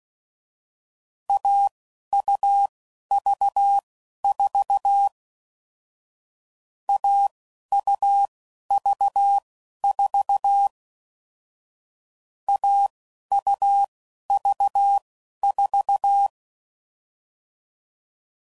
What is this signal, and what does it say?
Morse code, heard as a keyed tone, about 16 wpm, spaced as for 11 wpm, 800 Hz. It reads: AUV4 AUV4 AUV4
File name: Combinations / Difficult Combos / AUV AUV4